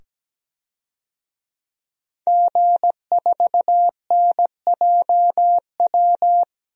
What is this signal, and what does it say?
Morse code, heard as a keyed tone, about 17 wpm, 700 Hz. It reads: G4NJW
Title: DXCC-100 Call Signs - 5 Characters - 17wpm